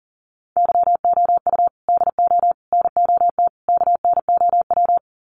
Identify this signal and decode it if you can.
Morse code, heard as a keyed tone, about 40 wpm, 700 Hz. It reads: YOU DO NOT KNOW
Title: Sets of 4 Words 40wpm